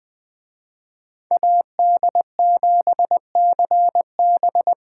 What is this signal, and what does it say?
Morse code, heard as a keyed tone, about 20 wpm, 700 Hz. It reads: AD7CB
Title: Speed Racing - USA 2x2 Call Signs 17wpm